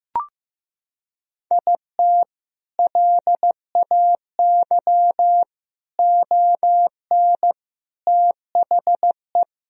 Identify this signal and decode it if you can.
Morse code, heard as a keyed tone, about 15 wpm, 700 Hz. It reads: IT LAY ON THE
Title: Sentences from Top 1000 Words - Review Part 2 - 15wpm